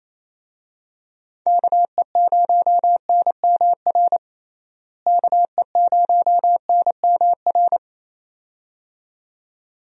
Morse code, heard as a keyed tone, about 28 wpm, 700 Hz. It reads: KE0NMR KE0NMR
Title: USA Call Signs - 6 Characters - Repeated 2x - 28wpm